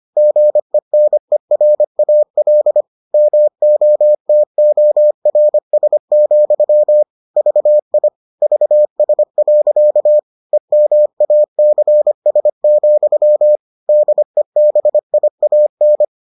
Morse code, heard as a keyed tone, about 25 wpm, 600 Hz. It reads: GENERAL MOTORS, VI VS. EMACS, DEBIAN